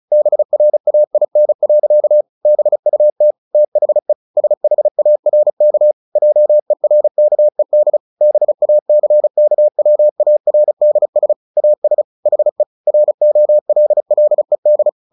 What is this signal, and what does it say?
Morse code, heard as a keyed tone, about 34 wpm, 600 Hz. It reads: BRAIN. BUT THE SHARK JERKED BACKWARDS AS HE ROLLED
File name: the old man and the sea 2 35wpm